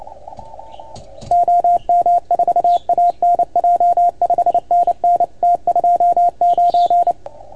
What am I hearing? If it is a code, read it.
Morse code, heard as a keyed tone, about 28 wpm, 690 Hz, with radio noise. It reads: OM4ANJ 5NN T29
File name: Archiv / CUC310 / OK1FGD CUC 310 pileup